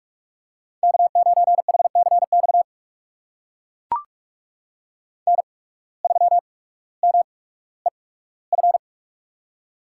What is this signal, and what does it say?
Morse code, heard as a keyed tone, about 43 wpm, spaced as for 19 wpm, 700 Hz. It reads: K9HCX N3MEF
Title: USA Call Signs - 5 Characters - 8x Character Spacing - 45wpm